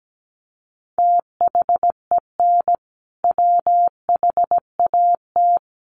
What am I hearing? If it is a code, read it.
Morse code, heard as a keyed tone, about 17 wpm, 700 Hz. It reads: THEN WHAT
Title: Sentences from Top 200 Words 17wpm